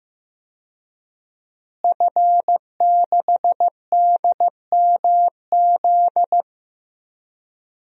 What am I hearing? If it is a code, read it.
Morse code, heard as a keyed tone, about 15 wpm, 700 Hz. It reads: F6DMZ
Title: DXCC-100 Call Signs - 5 Characters - Repeated 2x - 15wpm